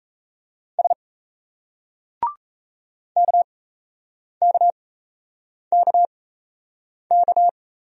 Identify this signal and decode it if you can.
Morse code, heard as a keyed tone, about 38 wpm, spaced as for 8 wpm, 700 Hz. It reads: S KKKK